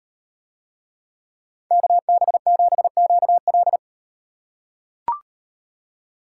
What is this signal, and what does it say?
Morse code, heard as a keyed tone, about 38 wpm, 700 Hz. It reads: KB7QL